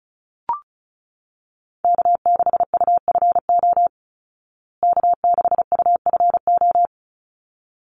Morse code, heard as a keyed tone, about 34 wpm, 700 Hz. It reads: K6UFO K6UFO
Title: Call Signs Repeated 3x - 35wpm